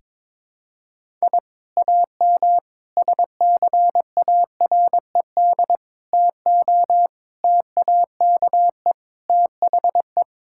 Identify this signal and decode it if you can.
Morse code, heard as a keyed tone, about 22 wpm, 700 Hz. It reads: I AM SCARED TO TAKE THE
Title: Sentences from Top 1000 Words - Review Part 5 - 22wpm